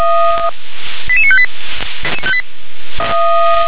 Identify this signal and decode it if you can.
Five-tone selective call (ZVEI), heard as quick runs of five tones, tones of about 70 ms. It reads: 88958
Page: SDR/ZVEI